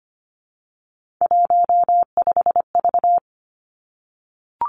Morse code, heard as a keyed tone, about 25 wpm, 700 Hz. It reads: 15V
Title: Three Challenging Letter-Number Combinations 25wpm